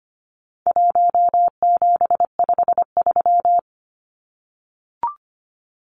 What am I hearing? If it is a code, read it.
Morse code, heard as a keyed tone, about 25 wpm, 700 Hz. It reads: 1753
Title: Numbers - 4-Digit 25wpm